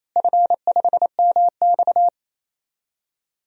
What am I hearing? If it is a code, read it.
Morse code, heard as a keyed tone, about 28 wpm, 700 Hz. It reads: F5MX
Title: DXCC-100 Call Signs - 4 Characters - Repeated 3x - 28wpm